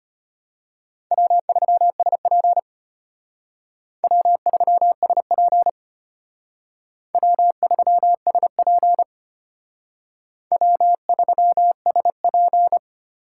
Morse code, heard as a keyed tone, about 33 wpm, 700 Hz. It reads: W3SP W3SP W3SP W3SP